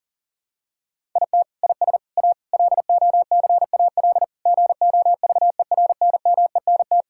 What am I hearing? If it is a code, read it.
Morse code, heard as a keyed tone, about 37 wpm, 700 Hz. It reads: IT IS A LOCAL GOVERNMENT
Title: Sentences from Top 1000 Words - Review Part 5 - 40wpm